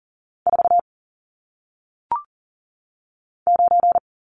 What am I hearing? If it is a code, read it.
Morse code, heard as a keyed tone, about 39 wpm, 700 Hz. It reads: 4 9
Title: Numbers - 1-Digit 40wpm